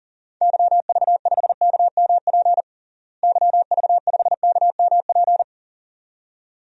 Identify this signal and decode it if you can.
Morse code, heard as a keyed tone, about 37 wpm, 700 Hz. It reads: YV5KMP YV5KMP
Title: DXCC-100 Call Signs - 6 Characters - Repeated 2x - 40wpm